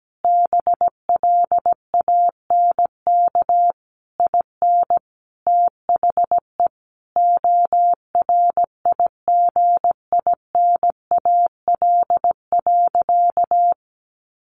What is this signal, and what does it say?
Morse code, heard as a keyed tone, about 17 wpm, 700 Hz. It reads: BLANK IN THE ORIGINAL.
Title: Sentences from Top 1500 Words 17wpm